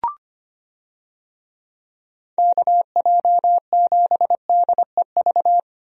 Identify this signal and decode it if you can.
Morse code, heard as a keyed tone, about 25 wpm, 700 Hz. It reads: KJ7DEV
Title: USA Call Signs - 6 Characters - Repeated 2x - 25wpm